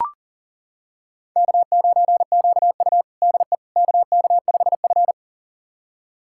Morse code, heard as a keyed tone, about 37 wpm, 700 Hz. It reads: K9QU DE KK5F
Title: USA Callsign DE Callsign - 4 Characters - 40wpm